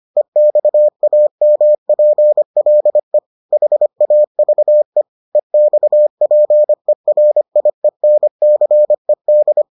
Morse code, heard as a keyed tone, about 25 wpm, 600 Hz. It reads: EXAMPLE HAVE EXPERIENCED